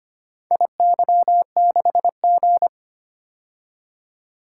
Morse code, heard as a keyed tone, about 25 wpm, 700 Hz. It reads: IY6G
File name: practice / 25wpm / Four Letter-Number Combinations - 25wpm